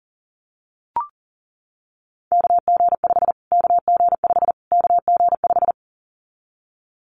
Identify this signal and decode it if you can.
Morse code, heard as a keyed tone, about 40 wpm, 700 Hz. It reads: KG5 KG5 KG5
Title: Callsign Prefixes for Top 25 DXCC - Repeated 3x - 40wpm